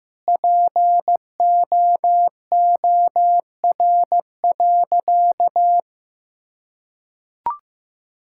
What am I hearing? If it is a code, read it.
Morse code, heard as a keyed tone, about 16 wpm, 700 Hz. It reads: POOR.
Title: Sentences from Top 1000 Words - Review Part 5 - 15wpm